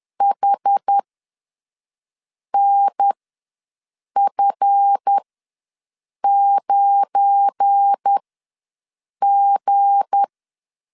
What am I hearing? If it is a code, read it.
Morse code, heard as a keyed tone, about 10 wpm, spaced as for 5 wpm, 800 Hz. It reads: HNF9G
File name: LEZIONI / 7 GF9Z / 7 GF9 ws 10wpm 4034